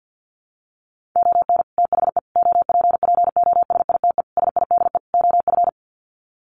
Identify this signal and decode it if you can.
Morse code, heard as a keyed tone, about 50 wpm, 700 Hz. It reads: ON THE OPPOSITE SIDE OF
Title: Sets of 5 Words Encore2 50wpm